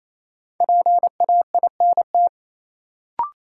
Morse code, heard as a keyed tone, about 28 wpm, 700 Hz. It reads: PAINT
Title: Sentences from Top 1000 Words - Review Part 6 - New Word Drill - 28wpm